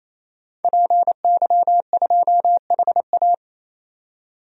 Morse code, heard as a keyed tone, about 28 wpm, 700 Hz. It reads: PY2HA
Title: DXCC-100 Call Signs - 5 Characters - 28wpm